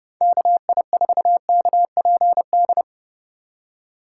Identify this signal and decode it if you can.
Morse code, heard as a keyed tone, about 30 wpm, 700 Hz. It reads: KI4KPD
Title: USA Call Signs - 6 Characters - 3x Character Spacing - 30wpm